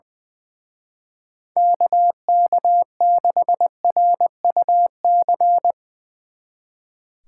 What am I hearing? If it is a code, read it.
Morse code, heard as a keyed tone, about 20 wpm, 700 Hz. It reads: KK6RUC
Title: Speed Racing - USA Call Signs 20wpm